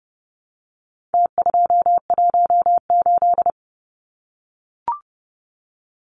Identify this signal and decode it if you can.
Morse code, heard as a keyed tone, about 30 wpm, 700 Hz. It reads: T218
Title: Numbers - Contest Style - 4-Digit 30wpm